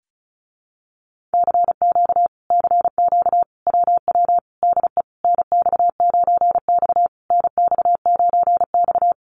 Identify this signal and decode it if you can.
Morse code, heard as a keyed tone, about 34 wpm, 700 Hz. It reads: CQ CQ WW DE NX9X NX9X
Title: Common QSO Phrases Encore 35wpm